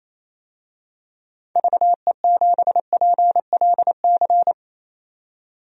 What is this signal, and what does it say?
Morse code, heard as a keyed tone, about 28 wpm, 700 Hz. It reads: VE7PLC
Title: DXCC-100 Call Signs - 6 Characters - 28wpm